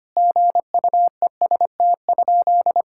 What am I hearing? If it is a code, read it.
Morse code, heard as a keyed tone, about 25 wpm, 700 Hz. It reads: GUEST?
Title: Sentences from Top 1600 Words 25wpm